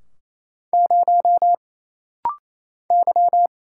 Morse code, heard as a keyed tone, about 27 wpm, 700 Hz. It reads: 0 Y
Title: Single Letter-Number - Rapid-Fire 28wpm